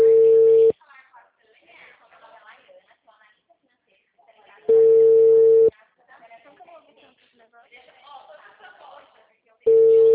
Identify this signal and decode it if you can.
Ringback tone (China, 450 Hz), with speech.
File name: out- -2010- - - .666